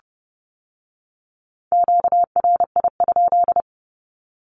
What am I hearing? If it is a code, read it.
Morse code, heard as a keyed tone, about 30 wpm, 700 Hz. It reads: QRI?
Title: Q-Codes with Spoken Meaning - 30wpm